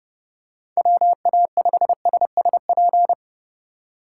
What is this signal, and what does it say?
Morse code, heard as a keyed tone, about 32 wpm, 700 Hz. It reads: WA5SSP
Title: USA Call Signs - 3x Character Spacing - 30wpm